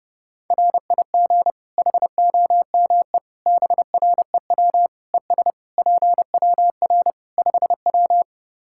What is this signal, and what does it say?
Morse code, heard as a keyed tone, about 30 wpm, 700 Hz. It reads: RIG HOME BREW ES PWR 5W
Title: Practice QSO - Exchange 2 - Info - 30wpm